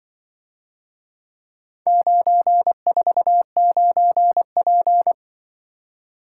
Morse code, heard as a keyed tone, about 24 wpm, 700 Hz. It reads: 949P
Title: Speed-Racing - Four Letter-Number Combinations - 20wpm